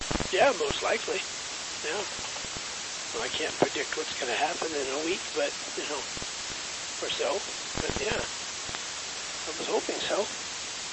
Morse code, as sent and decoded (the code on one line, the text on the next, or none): none